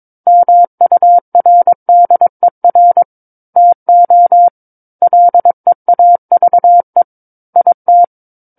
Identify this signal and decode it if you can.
Morse code, heard as a keyed tone, about 22 wpm, 700 Hz. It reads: MURDER TO LEAVE IT